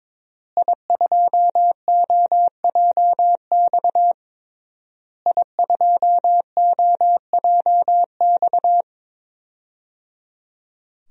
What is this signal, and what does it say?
Morse code, heard as a keyed tone, about 22 wpm, 700 Hz. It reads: I2OJX I2OJX
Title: DXCC-100 Call Signs - 5 Characters - Repeated 3x - 22wpm